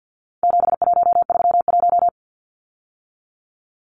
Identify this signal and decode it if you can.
Morse code, heard as a keyed tone, about 50 wpm, 700 Hz. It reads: 7131
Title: Numbers - 4-Digit 50wpm